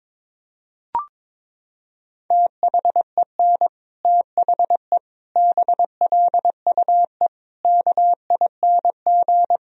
Morse code, heard as a keyed tone, about 22 wpm, 700 Hz. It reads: THEN THE BLUE KING
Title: Sentences from Top 1000 Words - Review Part 3 - 22wpm